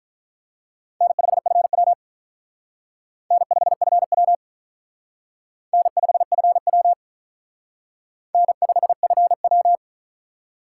Morse code, heard as a keyed tone, about 44 wpm, 700 Hz. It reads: N5FW N5FW N5FW N5FW